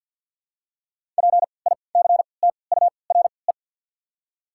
Morse code, heard as a keyed tone, about 53 wpm, spaced as for 29 wpm, 690 Hz. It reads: PICTURE
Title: 3x Character Spacing - Top 300 Words - 50wpm